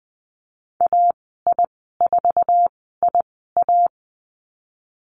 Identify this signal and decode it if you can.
Morse code, heard as a keyed tone, about 20 wpm, spaced as for 15 wpm, 700 Hz. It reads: AI4IA
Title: 069 Lesson 12g - Letter d - Call Signs 20wpm